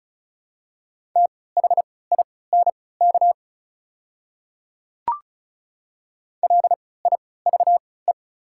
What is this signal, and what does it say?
Morse code, heard as a keyed tone, about 34 wpm, spaced as for 20 wpm, 700 Hz. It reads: THINK LIVE
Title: 3x Character Spacing - Top 200 Words - 35wpm